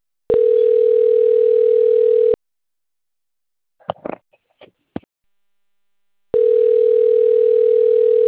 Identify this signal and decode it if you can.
Ringback tone (North America), 440 + 480 Hz.
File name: rg-1600-unknown- - - .84219